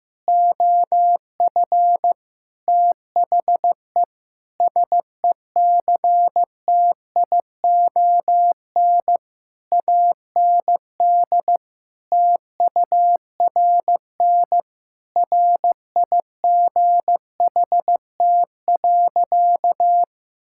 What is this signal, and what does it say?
Morse code, heard as a keyed tone, about 16 wpm, 700 Hz. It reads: OF THE SECTION AND TURN RIGHT.
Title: Sentences from Top 1000 Words - Review Part 5 - 15wpm